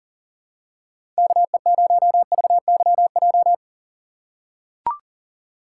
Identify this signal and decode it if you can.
Morse code, heard as a keyed tone, about 40 wpm, 700 Hz. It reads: KE0VYJ
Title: USA Call Signs - 6 Characters - 4x Character Spacing - 40wpm